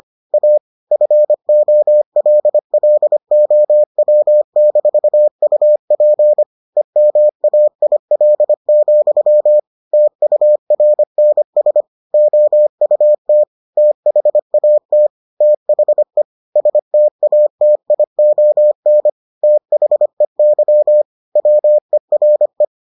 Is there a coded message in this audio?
Yes: Morse code from 0.3 to 22.6 s, heard as a keyed tone, about 25 wpm, 600 Hz. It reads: A FOLLOW-UP EMAIL, TURNS OUT THAT THE STATION THEY WERE